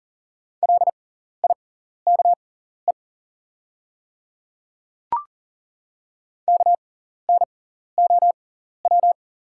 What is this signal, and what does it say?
Morse code, heard as a keyed tone, about 40 wpm, spaced as for 14 wpm, 690 Hz. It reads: LIKE KNOW
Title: 6x Character Spacing - Top 200 Words - 40wpm